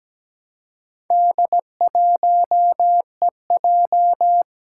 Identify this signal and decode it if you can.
Morse code, heard as a keyed tone, about 17 wpm, 700 Hz. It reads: D1EJ
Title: Four Challenging Letter-Number Combinations 17wpm